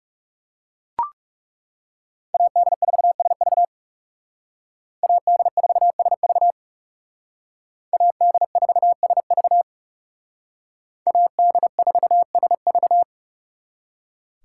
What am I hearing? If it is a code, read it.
Morse code, heard as a keyed tone, about 38 wpm, 700 Hz. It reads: AD4SV AD4SV AD4SV AD4SV